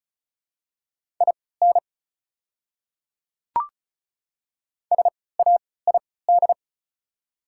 Morse code, heard as a keyed tone, about 34 wpm, spaced as for 20 wpm, 700 Hz. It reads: IN SAID
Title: 3x Character Spacing - Top 100 Words - 35wpm